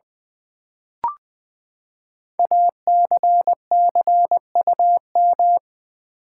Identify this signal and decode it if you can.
Morse code, heard as a keyed tone, about 20 wpm, 700 Hz. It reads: ACCUM
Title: Build A Word - 5 Letters At A Time - 12-Letter Words - 20wpm